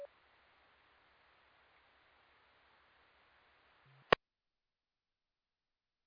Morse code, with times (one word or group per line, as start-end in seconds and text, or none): none